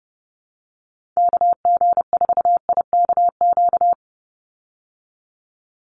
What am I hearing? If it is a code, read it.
Morse code, heard as a keyed tone, about 30 wpm, 700 Hz. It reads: KG4IKQ
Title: Call Signs Repeated 2x - 30wpm